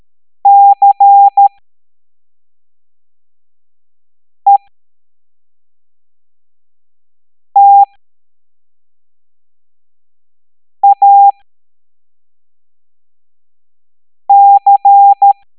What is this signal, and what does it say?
Morse code, heard as a keyed tone, about 13 wpm, spaced as for 3 wpm, 800 Hz. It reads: CETAC
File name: hamdata / L03-EX01-IGC